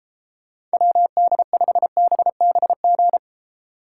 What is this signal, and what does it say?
Morse code, heard as a keyed tone, about 33 wpm, 700 Hz. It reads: WD5BBG